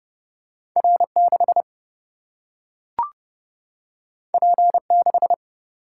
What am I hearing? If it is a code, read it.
Morse code, heard as a keyed tone, about 30 wpm, 700 Hz. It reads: R6 P6
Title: Single-Letter Single-Number 30wpm